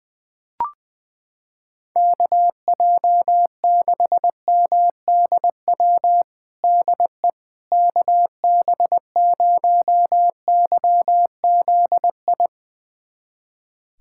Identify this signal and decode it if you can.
Morse code, heard as a keyed tone, about 19 wpm, 700 Hz. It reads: KJ6MDW DE KB0YZI
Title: USA Callsign DE Callsign 20wpm